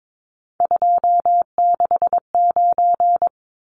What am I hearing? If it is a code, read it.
Morse code, heard as a keyed tone, about 22 wpm, 700 Hz. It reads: 269
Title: Numbers - 3-Digit 22wpm